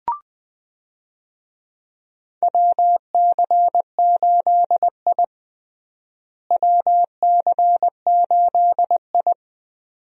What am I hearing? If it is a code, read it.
Morse code, heard as a keyed tone, about 20 wpm, 700 Hz. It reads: WC8I WC8I
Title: USA Call Signs - 4 Characters - Repeated 2x - 20wpm